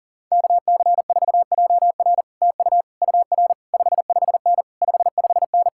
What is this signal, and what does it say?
Morse code, heard as a keyed tone, about 51 wpm, 700 Hz. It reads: KC4JR TU UR 55N 55N